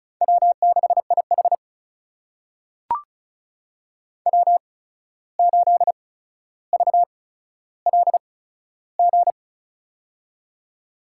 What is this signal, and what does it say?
Morse code, heard as a keyed tone, about 36 wpm, spaced as for 14 wpm, 700 Hz. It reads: W6IH W8VLG
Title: USA Call Signs - 8x Character Spacing - 35wpm